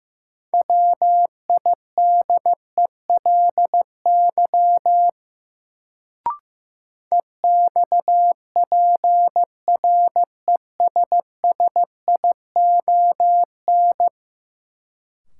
Morse code, heard as a keyed tone, about 15 wpm, 700 Hz. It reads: WIDELY EXPRESSION
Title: Sentences from Top 1200 Words - New Word Drill - 15wpm